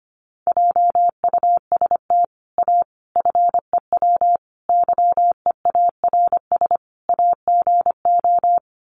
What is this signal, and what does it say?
Morse code, heard as a keyed tone, about 25 wpm, 690 Hz. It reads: JUST A FEW YEARS AGO
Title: Sets of 5 Words Encore 25wpm